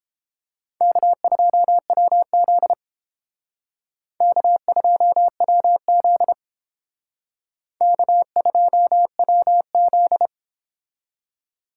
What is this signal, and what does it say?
Morse code, heard as a keyed tone, about 28 wpm, 700 Hz. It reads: K2WZ K2WZ K2WZ